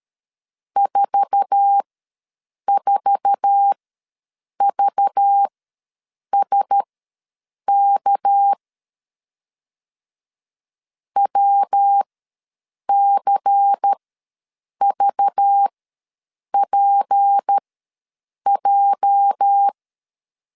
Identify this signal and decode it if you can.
Morse code, heard as a keyed tone, about 12 wpm, spaced as for 7 wpm, 800 Hz. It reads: 44VSK WCVPJ
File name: LEZIONI / 8 YJ45 / 8 YJ4 ws 12wpm 3249